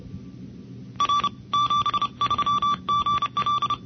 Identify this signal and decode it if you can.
Morse code, heard as a keyed tone, about 31 wpm, 1160 Hz. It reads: R 73 GL
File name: NCCC sweepstakes